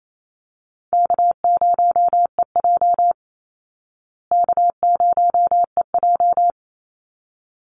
Morse code, heard as a keyed tone, about 28 wpm, 700 Hz. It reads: K0EJ K0EJ
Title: Easy Call Signs Repeated 2x - 28wpm